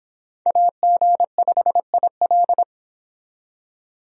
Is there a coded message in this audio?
Yes: Morse code from 0.5 to 2.6 s, heard as a keyed tone, about 25 wpm, 700 Hz. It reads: AG5IL